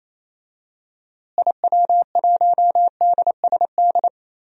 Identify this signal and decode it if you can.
Morse code, heard as a keyed tone, about 28 wpm, 700 Hz. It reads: IW1DSD
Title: DXCC-100 Call Signs - 6 Characters - Repeated 2x - 28wpm